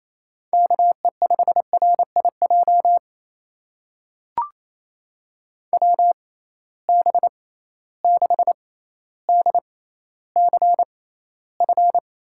Morse code, heard as a keyed tone, about 28 wpm, spaced as for 14 wpm, 700 Hz. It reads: KE5RIJ WB6DCF
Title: USA Call Signs - 6 Characters - 6x Character Spacing - 28wpm